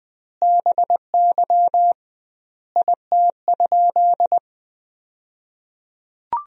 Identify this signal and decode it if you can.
Morse code, heard as a keyed tone, about 19 wpm, 700 Hz. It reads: BY IT?
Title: 2x Word-Spacing - Sentences from Top 200 Words - 20wpm